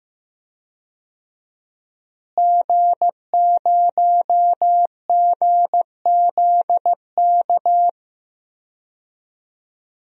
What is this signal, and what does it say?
Morse code, heard as a keyed tone, about 15 wpm, 700 Hz. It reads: G0GZK
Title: DXCC-100 Call Signs - 5 Characters - Repeated 3x - 15wpm